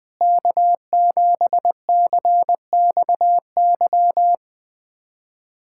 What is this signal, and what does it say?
Morse code, heard as a keyed tone, about 20 wpm, 700 Hz. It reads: K7CXY
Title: USA Callsign DE Callsign - 5 Characters - 20wpm